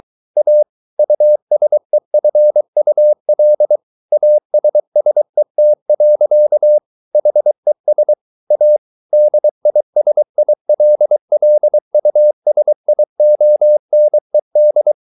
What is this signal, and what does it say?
Morse code, heard as a keyed tone, about 23 wpm, 600 Hz. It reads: A USEFUL ASSET. HES A DISILLUSIONED